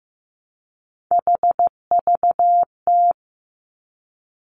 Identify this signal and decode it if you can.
Morse code, heard as a keyed tone, about 15 wpm, 700 Hz. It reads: HVT
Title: Three Challenging Letter-Number Combinations 15wpm